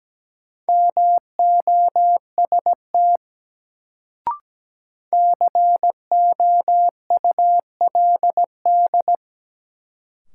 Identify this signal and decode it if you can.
Morse code, heard as a keyed tone, about 17 wpm, 700 Hz. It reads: MOST COULD
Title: Sets of 2 Words - New Word Drill - 17wpm